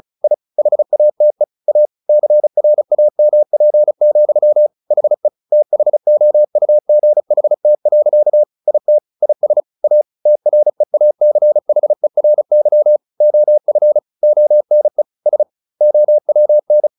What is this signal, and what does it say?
Morse code, heard as a keyed tone, about 34 wpm, 600 Hz. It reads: I HATE A CRAMP, HE THOUGHT. IT IS A TREACHERY OF ONE S OWN
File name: the old man and the sea / the old man and the sea 1 35wpm